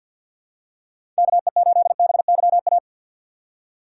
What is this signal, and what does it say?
Morse code, heard as a keyed tone, about 48 wpm, 700 Hz. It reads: KE8BYA
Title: USA Call Signs - 6 Characters - 3x Character Spacing - 50wpm